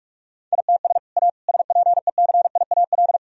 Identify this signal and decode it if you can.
Morse code, heard as a keyed tone, about 42 wpm, 700 Hz. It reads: ITS A SPECIAL
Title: Sentences from Top 1000 Words - Review Part 3 - 45wpm